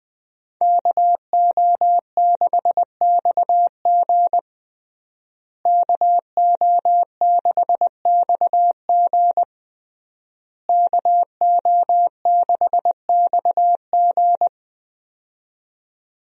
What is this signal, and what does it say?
Morse code, heard as a keyed tone, about 20 wpm, 700 Hz. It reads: KO6XG KO6XG KO6XG
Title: USA Call Signs - 5 Characters - Repeated 3x - 20wpm